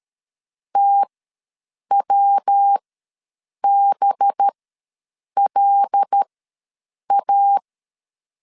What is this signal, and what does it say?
Morse code, heard as a keyed tone, about 13 wpm, spaced as for 7 wpm, 800 Hz. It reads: TWBLA
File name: 5 LB ws 12wpm